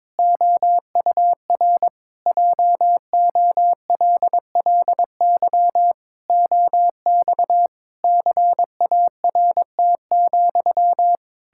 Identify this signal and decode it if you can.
Morse code, heard as a keyed tone, about 22 wpm, 700 Hz. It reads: OUR JOLLY OX CART,
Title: Pangram Challenge v2 - 22wpm